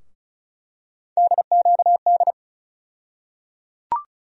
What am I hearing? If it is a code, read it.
Morse code, heard as a keyed tone, about 36 wpm, 700 Hz. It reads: DQD
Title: USA Call Sign Postfixes - 3 Letters 35wpm